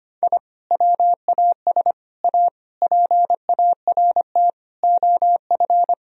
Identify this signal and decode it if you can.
Morse code, heard as a keyed tone, about 25 wpm, 700 Hz. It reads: I WAS A PART OF
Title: Sentences from Top 1000 Words - Review Part 1 - 25wpm